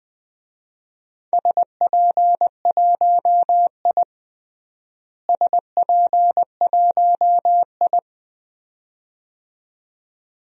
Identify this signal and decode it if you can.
Morse code, heard as a keyed tone, about 20 wpm, 700 Hz. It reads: SP1I SP1I
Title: DXCC-100 Call Signs - 4 Characters - Repeated 2x - 20wpm